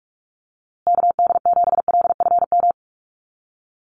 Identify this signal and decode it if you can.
Morse code, heard as a keyed tone, about 44 wpm, 700 Hz. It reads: KD7LFM
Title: USA Call Signs Repeated 2x 45wpm